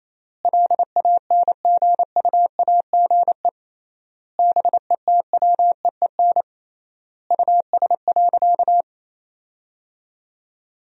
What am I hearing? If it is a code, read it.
Morse code, heard as a keyed tone, about 28 wpm, 700 Hz. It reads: LANGUAGE BETWEEN US.
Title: 3x Word-Spacing - Sentences from Top 500 Words - 28wpm